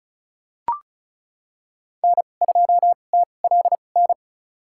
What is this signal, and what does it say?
Morse code, heard as a keyed tone, about 36 wpm, spaced as for 25 wpm, 700 Hz. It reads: N2TLN
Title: USA Call Signs - 2x Character Spacing - 35wpm